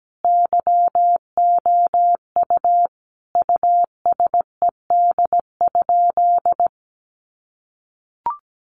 Morse code, heard as a keyed tone, about 17 wpm, 700 Hz. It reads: YOU USED?
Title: Sentences from Top 1200 Words 17wpm